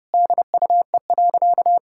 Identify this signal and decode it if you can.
Morse code, heard as a keyed tone, about 32 wpm, 700 Hz. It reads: DUE.
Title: Sentences from Top 1000 Words - Review Part 6 - 30wpm